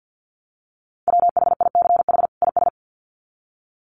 Morse code, heard as a keyed tone, about 50 wpm, 700 Hz. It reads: WHICH IS
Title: Sets of 2 Words 50wpm